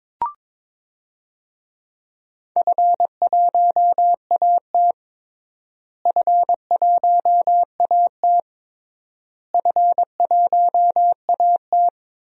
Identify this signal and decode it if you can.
Morse code, heard as a keyed tone, about 22 wpm, 700 Hz. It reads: F1AT F1AT F1AT
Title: DXCC-100 Call Signs - 4 Characters - Repeated 3x - 22wpm